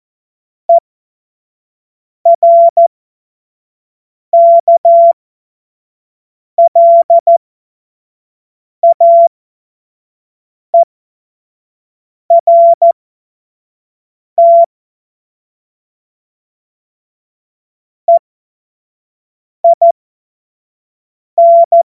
Morse code, heard as a keyed tone, about 14 wpm, spaced as for 5 wpm, 680 Hz. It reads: ERKLAERT EIN
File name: Morsekurs / Witze / Witz 20. Januar 2026 25 F70